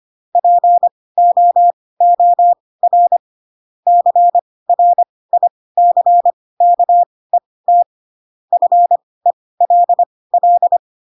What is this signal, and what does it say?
Morse code, heard as a keyed tone, about 25 wpm, spaced as for 18 wpm, 700 Hz. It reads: POOR CRICKET FELL
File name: Pinocchio / Pinocchio ch4 12 25 ik1hge